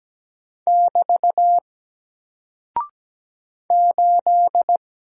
Morse code, heard as a keyed tone, about 17 wpm, 700 Hz. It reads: <BT> 8
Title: Single Character (Letter:Number:Punctuation:Prosign) Alt 17wpm